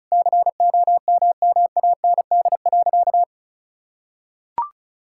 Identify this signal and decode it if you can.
Morse code, heard as a keyed tone, about 36 wpm, 700 Hz. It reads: COMMAND.
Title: Sentences from Top 1600 Words 35wpm